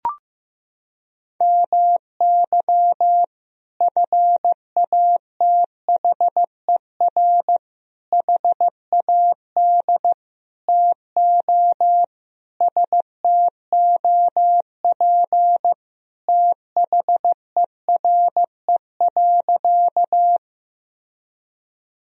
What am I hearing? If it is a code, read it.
Morse code, heard as a keyed tone, about 16 wpm, 700 Hz. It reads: MY FATHER HAD TO STOP THERE.
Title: Sentences from Top 1000 Words - Review Part 1 - 15wpm